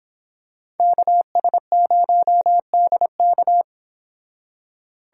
Morse code, heard as a keyed tone, about 26 wpm, 700 Hz. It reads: KS0DK